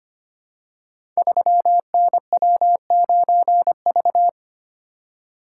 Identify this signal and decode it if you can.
Morse code, heard as a keyed tone, about 25 wpm, 700 Hz. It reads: 3NW9V
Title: Code Groups - 5 Letter:Numbers - 25wpm